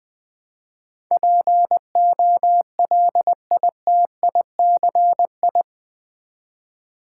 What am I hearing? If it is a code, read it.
Morse code, heard as a keyed tone, about 20 wpm, 700 Hz. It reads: POLITICI
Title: Build A Word - 4 Letters At A Time - 10-Letter Words - 20wpm